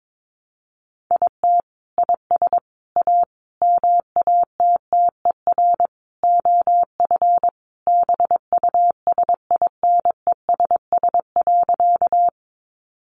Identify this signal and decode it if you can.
Morse code, heard as a keyed tone, about 22 wpm, 700 Hz. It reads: IT IS A MATTER OF BUSINESS.
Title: Sentences from Top 700 Words 22wpm